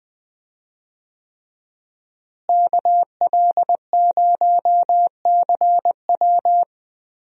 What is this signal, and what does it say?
Morse code, heard as a keyed tone, about 20 wpm, 700 Hz. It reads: KL0CW
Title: USA Call Signs - 5 Characters - Repeated 3x - 20wpm